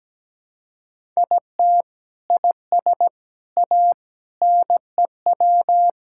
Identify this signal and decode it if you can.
Morse code, heard as a keyed tone, about 17 wpm, 700 Hz. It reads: IT IS A NEW
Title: Sentences from Top 1000 Words - Review Part 5 - 17wpm